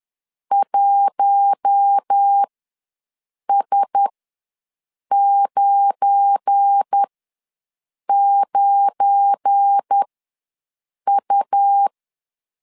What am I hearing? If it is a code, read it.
Morse code, heard as a keyed tone, about 10 wpm, spaced as for 6 wpm, 800 Hz. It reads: 1S99U
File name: LEZIONI / 7 GF9Z / 7 GF9 ws 10wpm 4034